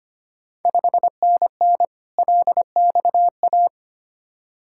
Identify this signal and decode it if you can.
Morse code, heard as a keyed tone, about 25 wpm, 700 Hz. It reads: 5NN LXA
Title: Ontario QSO Party Exchange 25wpm